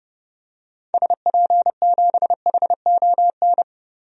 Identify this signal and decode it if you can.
Morse code, heard as a keyed tone, about 30 wpm, 700 Hz. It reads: SP7HON
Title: DXCC-100 Call Signs - 6 Characters - Repeated 2x - 30wpm